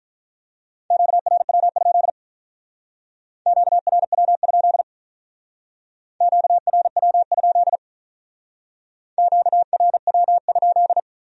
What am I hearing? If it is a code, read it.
Morse code, heard as a keyed tone, about 44 wpm, 700 Hz. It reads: QRW? QRW? QRW? QRW?